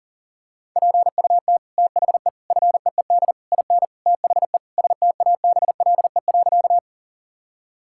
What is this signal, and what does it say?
Morse code, heard as a keyed tone, about 40 wpm, 700 Hz. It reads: PUT THE FEED IN THE STABLE.
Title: Sentences from Top 1600 Words 40wpm